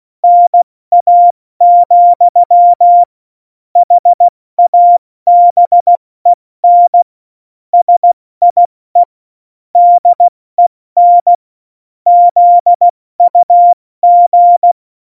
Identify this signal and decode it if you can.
Morse code, heard as a keyed tone, about 16 wpm, 700 Hz. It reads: NA, HABEN SIE DEN ZUG
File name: Morsekurs / Witze / Witz 11 Juli 17, Tempo 70 F80